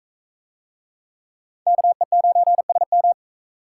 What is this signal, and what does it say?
Morse code, heard as a keyed tone, about 42 wpm, 700 Hz. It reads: KE9SM